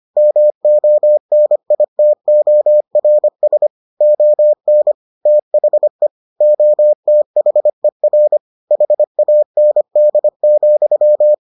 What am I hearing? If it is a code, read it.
Morse code, heard as a keyed tone, about 25 wpm, 600 Hz. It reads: MONITORS ON THE OTHER HAND,